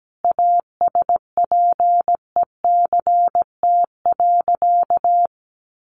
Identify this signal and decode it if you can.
Morse code, heard as a keyed tone, about 17 wpm, 700 Hz. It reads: ASPECT.
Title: Sentences from Top 1300 Words 17wpm